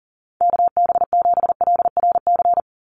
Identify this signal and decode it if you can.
Morse code, heard as a keyed tone, about 40 wpm, 700 Hz. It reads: KB7LRC
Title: USA Call Signs 40wpm